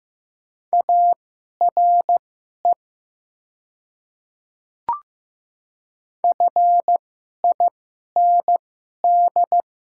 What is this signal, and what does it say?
Morse code, heard as a keyed tone, about 15 wpm, spaced as for 11 wpm, 700 Hz. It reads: ARE FIND
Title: 2x Character Spacing - Top 200 Words - 15wpm